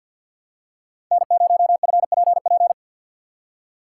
Morse code, heard as a keyed tone, about 49 wpm, 700 Hz. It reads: N0FPP
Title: USA Call Signs - 5 Characters - 2x Character Spacing - 50wpm